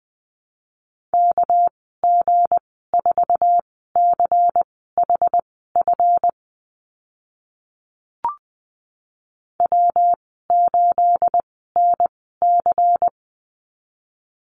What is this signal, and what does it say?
Morse code, heard as a keyed tone, about 20 wpm, spaced as for 14 wpm, 700 Hz. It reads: KG4CHF W8NC
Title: 219 Lesson 35g - Number 0 - Call Signs 20wpm